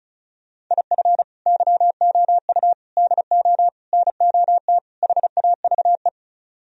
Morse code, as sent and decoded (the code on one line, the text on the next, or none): .. ..-. / -.-- --- ..- / -.. --- / -. --- - / .... .- ...- .
IF YOU DO NOT HAVE